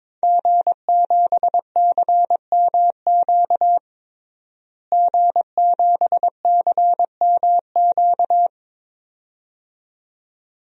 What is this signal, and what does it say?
Morse code, heard as a keyed tone, about 22 wpm, 700 Hz. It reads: G7CMQ G7CMQ
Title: DXCC-100 Call Signs - 5 Characters - Repeated 2x - 22wpm